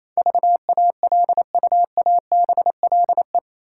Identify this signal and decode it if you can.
Morse code, heard as a keyed tone, about 28 wpm, 700 Hz. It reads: VALUABLE
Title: Sentences from Top 1100 Words - New Word Drill Alt - 28wpm